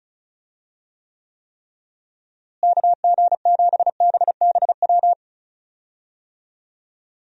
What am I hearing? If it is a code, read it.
Morse code, heard as a keyed tone, about 34 wpm, 700 Hz. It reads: KG7BBW
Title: USA Call Signs - 6 Characters - Repeated 3x - 35wpm